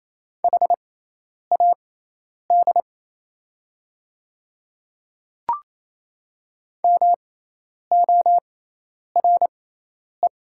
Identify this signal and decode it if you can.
Morse code, heard as a keyed tone, about 28 wpm, spaced as for 10 wpm, 700 Hz. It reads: HAD MORE
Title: 6x Character Spacing - Top 100 Words - 28wpm